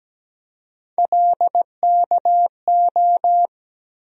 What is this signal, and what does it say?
Morse code, heard as a keyed tone, about 17 wpm, 700 Hz. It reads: LKO
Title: USA Call Sign Postfixes - 3 Letters 17wpm